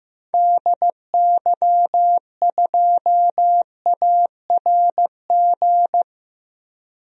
Morse code, heard as a keyed tone, about 15 wpm, 700 Hz. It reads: DY2ARG
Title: DXCC-100 Call Signs - 6 Characters - Repeated 3x - 15wpm